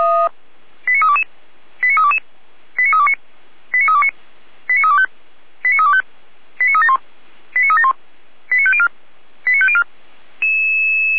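Five-tone selective call (ZVEI), heard as quick runs of five tones, tones of about 70 ms. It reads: 89322 89322 89329 89329 89325 89325 89371 89371 89504 89504
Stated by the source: ZVEI five-tone call received by SDR, from feuerwehr-orlamuende.org (Orlamünde fire brigade).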